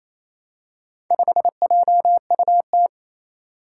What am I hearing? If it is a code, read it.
Morse code, heard as a keyed tone, about 29 wpm, 700 Hz. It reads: 5JUT
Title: Four Challenging Letter-Number Combinations 28wpm